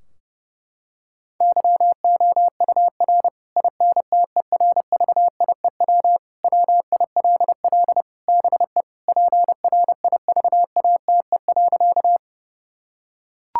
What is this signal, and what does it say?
Morse code, heard as a keyed tone, about 30 wpm, 700 Hz. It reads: YOUR INTERVIEW WILL BE PRIVATE.
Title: Sentences from Top 1600 Words 30wpm